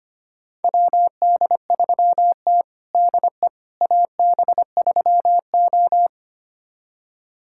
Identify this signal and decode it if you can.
Morse code, heard as a keyed tone, about 25 wpm, 700 Hz. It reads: WD3T DE AB3O
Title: USA Callsign DE Callsign - 4 Characters - 25wpm